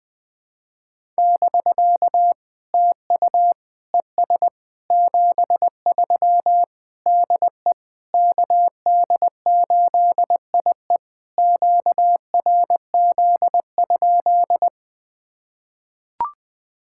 Morse code, heard as a keyed tone, about 20 wpm, 690 Hz. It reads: <BK> TU ES 73 DE KD8IE QRZ?
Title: POTA QSO Practice - 20wpm